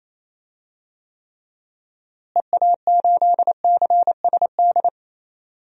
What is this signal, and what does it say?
Morse code, heard as a keyed tone, about 28 wpm, 700 Hz. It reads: EA8CSD
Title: DXCC-100 Call Signs - 6 Characters - Repeated 3x - 28wpm